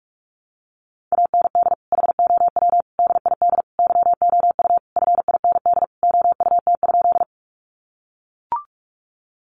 Morse code, heard as a keyed tone, about 44 wpm, 700 Hz. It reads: AND HOW DID YOU FIND OUT?
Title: Sentences from Top 200 Words 45wpm